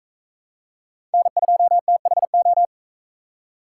Morse code, heard as a keyed tone, about 38 wpm, 700 Hz. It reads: N2THO